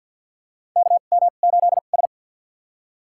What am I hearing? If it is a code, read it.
Morse code, heard as a keyed tone, about 51 wpm, spaced as for 36 wpm, 700 Hz. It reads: KM8S
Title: USA Call Signs - 4 Characters - 2x Character Spacing - 50wpm